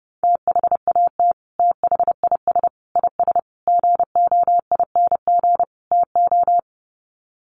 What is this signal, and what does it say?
Morse code, heard as a keyed tone, about 30 wpm, 700 Hz. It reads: THAT THIS IS GOING TO
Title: Sets of 5 Words Encore2 30wpm